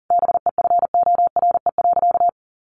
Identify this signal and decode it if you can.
Morse code, heard as a keyed tone, about 40 wpm, 700 Hz. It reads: BEFORE.
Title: Sentences from Top 1300 Words 40wpm